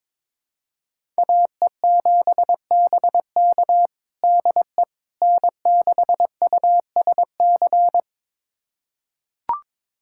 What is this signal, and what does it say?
Morse code, heard as a keyed tone, about 22 wpm, 700 Hz. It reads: AE7BK DE N6USC
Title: USA Callsign DE Callsign - 5 Characters - 22wpm